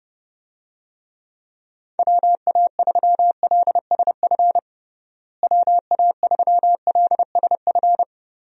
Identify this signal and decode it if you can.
Morse code, heard as a keyed tone, about 30 wpm, 700 Hz. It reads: WA3LSF WA3LSF
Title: USA Call Signs - 6 Characters - Repeated 2x - 30wpm